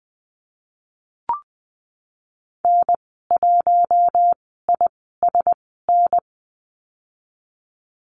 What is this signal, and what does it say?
Morse code, heard as a keyed tone, about 20 wpm, spaced as for 15 wpm, 700 Hz. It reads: N1ISN
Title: 055 Lesson 10g - Letter r - Call Signs 20wpm